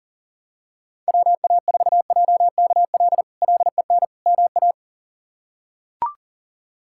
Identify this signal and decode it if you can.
Morse code, heard as a keyed tone, about 40 wpm, 700 Hz. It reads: WA4JKL LEN MA
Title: K1USN SST QSO Practice - 40wpm